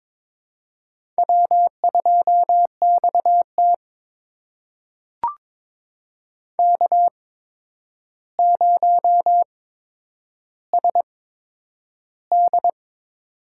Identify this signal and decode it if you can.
Morse code, heard as a keyed tone, about 22 wpm, spaced as for 9 wpm, 700 Hz. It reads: W2XT K0SD
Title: USA Call Signs - 4 Characters - 8x Character Spacing - 22wpm